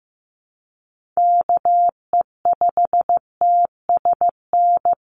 Morse code, heard as a keyed tone, about 15 wpm, 700 Hz. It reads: KE5TSN
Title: USA Call Signs 15wpm